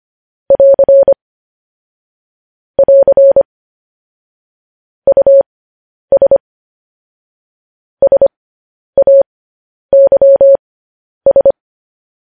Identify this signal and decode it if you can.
Morse code, heard as a keyed tone, about 25 wpm, spaced as for 10 wpm, 550 Hz. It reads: <AR> <AR> US SAYS